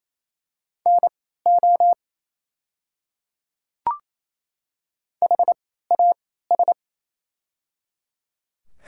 Morse code, heard as a keyed tone, about 27 wpm, spaced as for 16 wpm, 700 Hz. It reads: NO HAS
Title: 3x Character Spacing - Top 300 Words - 28wpm